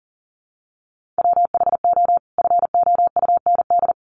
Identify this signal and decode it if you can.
Morse code, heard as a keyed tone, about 40 wpm, 700 Hz. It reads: WHO FOUND
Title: Sentences from Top 300 Words 40wpm